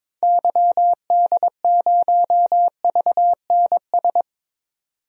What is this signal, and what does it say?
Morse code, heard as a keyed tone, about 22 wpm, 700 Hz. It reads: YD0VNS
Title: DXCC-100 Call Signs - 6 Characters - 22wpm